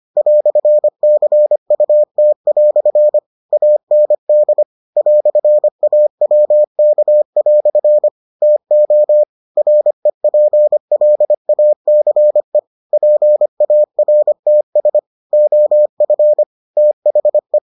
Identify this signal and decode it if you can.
Morse code, heard as a keyed tone, about 25 wpm, 600 Hz. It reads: "CUT" AND "AWK" TO REPLACE PARTS OF THE